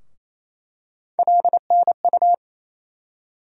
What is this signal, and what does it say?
Morse code, heard as a keyed tone, about 28 wpm, 700 Hz. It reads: LNU
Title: USA Call Sign Postfixes - 3 Letters 28wpm